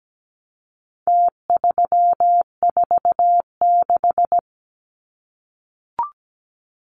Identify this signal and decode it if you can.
Morse code, heard as a keyed tone, about 17 wpm, 700 Hz. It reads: T346
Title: Numbers - Contest Style - 4-Digit 17wpm